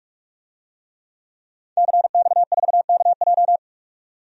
Morse code, heard as a keyed tone, about 41 wpm, 700 Hz. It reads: CX4KJ
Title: DXCC-100 Call Signs - 5 Characters - 45wpm